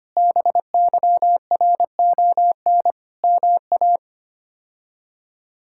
Morse code, heard as a keyed tone, about 25 wpm, 700 Hz. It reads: BYRON MA
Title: K1USN SST QSO Practice - 25wpm